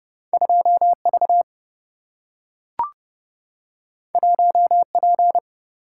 Morse code, heard as a keyed tone, about 30 wpm, 700 Hz. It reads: 2 V 1 P
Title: Single-Letter Single-Number 30wpm